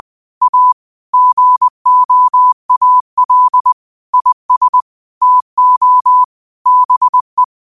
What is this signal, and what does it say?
Morse code, heard as a keyed tone, about 20 wpm, 1000 Hz. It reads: A GOAL IS TO BE
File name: qotdfiles / 20wpm